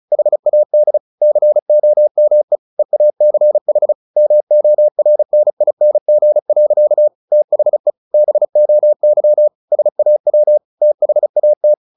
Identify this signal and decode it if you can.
Morse code, heard as a keyed tone, about 34 wpm, 600 Hz. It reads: HAD COME EACH MORNING. THE BOY SAW THAT
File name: the old man and the sea 2 35wpm